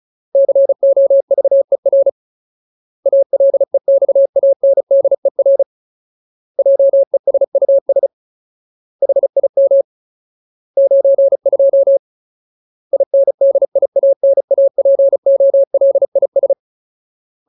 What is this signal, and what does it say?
Morse code, heard as a keyed tone, about 34 wpm, 550 Hz. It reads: COVER ALEXANDER JESUS HIM 92 INDIANAPOLIS
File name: Ultimate Word List / 35WPM Ultimate